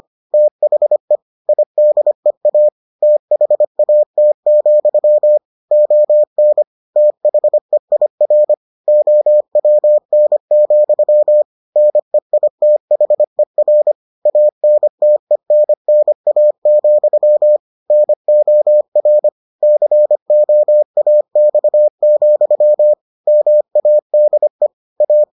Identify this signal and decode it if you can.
Morse code, heard as a keyed tone, about 25 wpm, 600 Hz. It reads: THE IDEA THAT, ON THEIR OWN, NEITHER ANTENNA, NOR COAX, MADE A